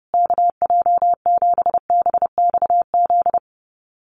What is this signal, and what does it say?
Morse code, heard as a keyed tone, about 28 wpm, 700 Hz. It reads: KJ7BXZ
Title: USA Call Signs Repeated 3x 30wpm